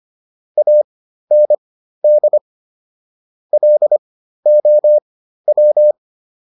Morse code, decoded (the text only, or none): AND LOW